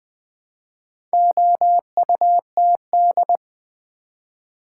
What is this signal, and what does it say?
Morse code, heard as a keyed tone, about 20 wpm, 700 Hz. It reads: OUTD
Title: Build A Word - 2 Letters At A Time - 8-Letter Words - 20wpm